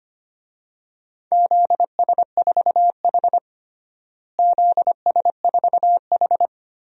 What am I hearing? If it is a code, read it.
Morse code, heard as a keyed tone, about 25 wpm, 700 Hz. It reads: ZS4H ZS4H
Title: DXCC-100 Call Signs - 4 Characters - Repeated 3x - 25wpm